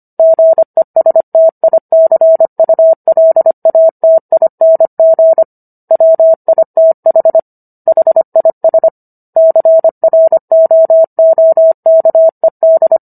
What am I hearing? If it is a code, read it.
Morse code, heard as a keyed tone, about 25 wpm, 650 Hz. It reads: GESTICULATING WITH HIS CROOKED